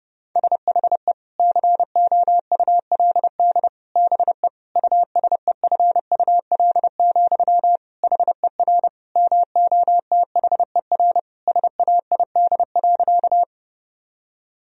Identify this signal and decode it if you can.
Morse code, heard as a keyed tone, about 29 wpm, 700 Hz. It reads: SHE COULD BE USEFUL, HER MOTHER SAID.
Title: Sentences from Top 1000 Words - Review Part 4 - 30wpm